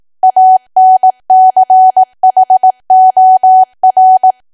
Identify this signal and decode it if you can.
Morse code, heard as a keyed tone, about 18 wpm, 750 Hz. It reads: ANCHOR
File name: Archive / 18 WPM / 18WPM